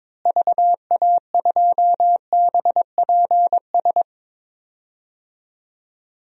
Morse code, heard as a keyed tone, about 22 wpm, 700 Hz. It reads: VA2BPS
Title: DXCC-100 Call Signs - 6 Characters - Repeated 2x - 22wpm